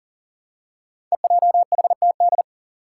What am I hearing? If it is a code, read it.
Morse code, heard as a keyed tone, about 37 wpm, 700 Hz. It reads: EJHTD
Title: Five Challenging Letter-Number Combinations 40wpm